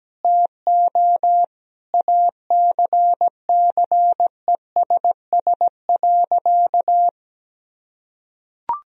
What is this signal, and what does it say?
Morse code, heard as a keyed tone, about 17 wpm, 700 Hz. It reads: TO ACCESS.
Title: Sentences from Top 1000 Words - Review Part 4 - 17wpm